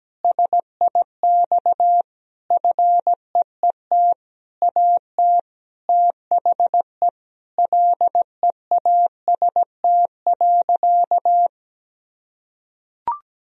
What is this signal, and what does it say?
Morse code, heard as a keyed tone, about 17 wpm, 700 Hz. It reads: SIX FEET AT THE LEAST.
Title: Sentences from Top 1000 Words - Review Part 2 - 17wpm